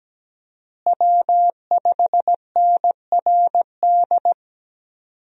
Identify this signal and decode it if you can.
Morse code, heard as a keyed tone, about 17 wpm, 700 Hz. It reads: W5NRD
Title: USA Call Signs - 5 Characters - Repeated 2x - 17wpm